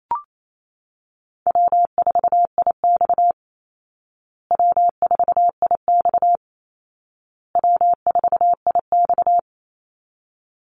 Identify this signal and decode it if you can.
Morse code, heard as a keyed tone, about 28 wpm, 700 Hz. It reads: W4IX W4IX W4IX
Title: Easy Call Signs Repeated 3x - 28wpm